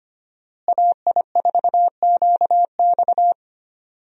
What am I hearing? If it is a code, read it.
Morse code, heard as a keyed tone, about 25 wpm, 700 Hz. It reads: AI4QX
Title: USA Call Signs - 5 Characters - Repeated 2x - 25wpm